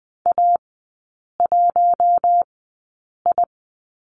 Morse code, heard as a keyed tone, about 19 wpm, spaced as for 8 wpm, 690 Hz. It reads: A1I
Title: 060 Lesson 11e - Letter h - Three Characters 20wpm